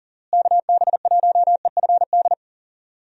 Morse code, heard as a keyed tone, about 40 wpm, 700 Hz. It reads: KB1EFD
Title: USA Call Signs - 6 Characters - Repeated 2x - 40wpm